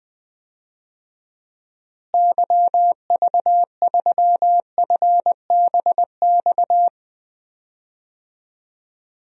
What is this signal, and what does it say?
Morse code, heard as a keyed tone, about 21 wpm, 700 Hz. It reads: YV3FBX
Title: DXCC-100 Call Signs - 6 Characters - Repeated 3x - 20wpm